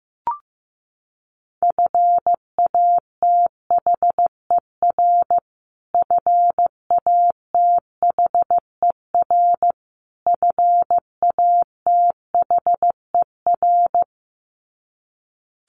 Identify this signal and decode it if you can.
Morse code, heard as a keyed tone, about 15 wpm, 700 Hz. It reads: FATHER FATHER FATHER
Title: Top 200 Words Repeated 15wpm